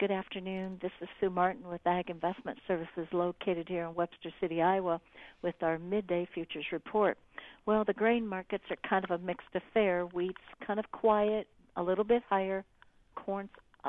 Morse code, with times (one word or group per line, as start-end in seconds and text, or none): none